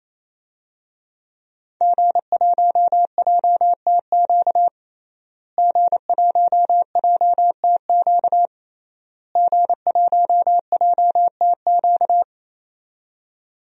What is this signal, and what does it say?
Morse code, heard as a keyed tone, about 28 wpm, 700 Hz. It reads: G1JTQ G1JTQ G1JTQ
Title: DXCC-100 Call Signs - 5 Characters - Repeated 3x - 28wpm